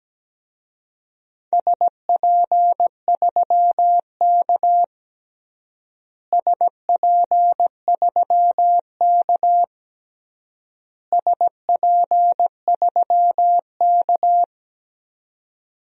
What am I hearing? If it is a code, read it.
Morse code, heard as a keyed tone, about 17 wpm, 700 Hz. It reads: SP3K SP3K SP3K
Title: DXCC-100 Call Signs - 4 Characters - Repeated 3x - 17wpm